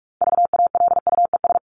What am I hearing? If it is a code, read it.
Morse code, heard as a keyed tone, about 44 wpm, 700 Hz. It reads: VALUES
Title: ES:S Suffix - Top 100 Words with Base Words 45wpm